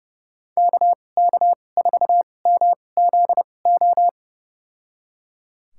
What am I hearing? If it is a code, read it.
Morse code, heard as a keyed tone, about 31 wpm, spaced as for 22 wpm, 700 Hz. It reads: KK4MZO
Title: USA Call Signs - 6 Characters - 2x Character Spacing - 30wpm